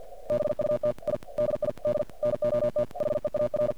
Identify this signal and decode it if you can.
Morse code, heard as a keyed tone, about 48 wpm, 590 Hz. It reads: BUT I DID NOT HEAR
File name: archived / CRSnet